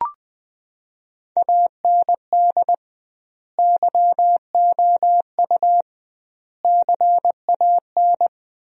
Morse code, heard as a keyed tone, about 21 wpm, 700 Hz. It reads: AND YOU CAN
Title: 2x Word-Spacing - Sentences from Top 400 Words - 20wpm